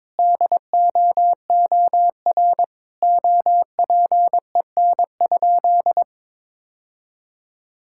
Morse code, heard as a keyed tone, about 22 wpm, 700 Hz. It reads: DOOR OPEN?
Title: Sentences from Top 1600 Words 22wpm